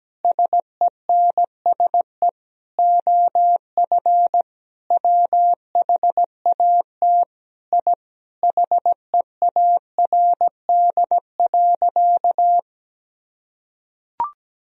Morse code, heard as a keyed tone, about 17 wpm, 700 Hz. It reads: SENSE OF WHAT I HEARD.
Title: Sentences from Top 1000 Words - Review Part 4 - 17wpm